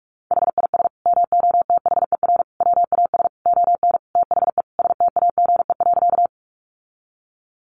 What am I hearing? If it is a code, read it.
Morse code, heard as a keyed tone, about 44 wpm, 700 Hz. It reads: HIS MOTHER WAS ON THE STAGE.
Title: Sentences from Top 1200 Words 45wpm